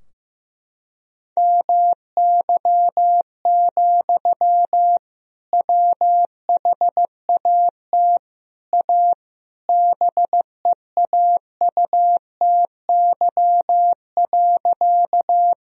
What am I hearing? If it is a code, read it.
Morse code, heard as a keyed tone, about 15 wpm, 700 Hz. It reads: MY, WHAT A BEAUTY.
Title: Sentences from Top 1000 Words - Review Part 2 - 15wpm